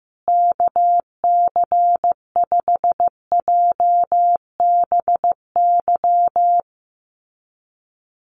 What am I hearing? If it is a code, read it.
Morse code, heard as a keyed tone, about 15 wpm, 700 Hz. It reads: KC5JBY
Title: USA Call Signs Repeated 3x 15wpm